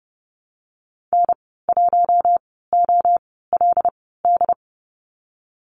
Morse code, heard as a keyed tone, about 30 wpm, spaced as for 17 wpm, 700 Hz. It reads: N1OLD
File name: Lesson 14g - Number 2 - Call Signs 30wpm